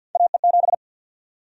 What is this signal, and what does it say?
Morse code, heard as a keyed tone, about 52 wpm, 700 Hz. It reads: AE7
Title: USA Call Sign Prefixes - 50wpm